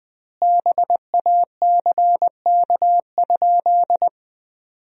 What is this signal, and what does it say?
Morse code, heard as a keyed tone, about 20 wpm, 700 Hz. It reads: BACK?
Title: 2x Word-Spacing - Sentences from Top 400 Words - 20wpm